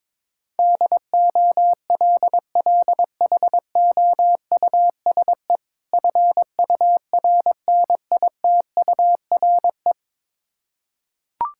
Morse code, heard as a keyed tone, about 22 wpm, 690 Hz. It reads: DOLLHOUSE FURNITURE
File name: Difficult Words with Hints - 22wpm